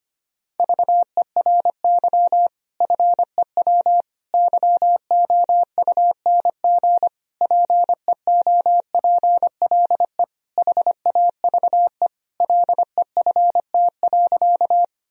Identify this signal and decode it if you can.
Morse code, heard as a keyed tone, about 25 wpm, 700 Hz. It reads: VERY FEW YOUNG PEOPLE HAVE LEFT.
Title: Sentences from Top 1000 Words - Review Part 1 - 25wpm